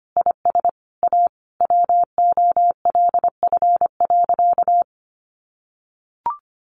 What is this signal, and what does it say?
Morse code, heard as a keyed tone, about 25 wpm, 700 Hz. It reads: IS A WOLF.
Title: Additional Easy Phrases 25wpm